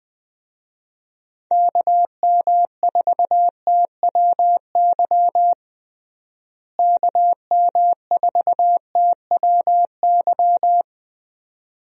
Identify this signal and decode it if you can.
Morse code, heard as a keyed tone, about 20 wpm, 700 Hz. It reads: KM4TWY KM4TWY
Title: USA Call Signs - 6 Characters - Repeated 2x - 20wpm